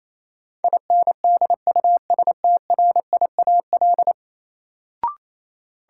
Morse code, heard as a keyed tone, about 28 wpm, 700 Hz. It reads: INDUSTRIAL
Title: Sets of 5 Words - New Word Drill - 28wpm